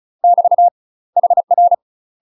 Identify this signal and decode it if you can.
Morse code, heard as a keyed tone, about 34 wpm, 700 Hz. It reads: <BT> HR
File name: UK MASTER QSOs TRAINING / 35WPM / UK MASTER QSOs 0043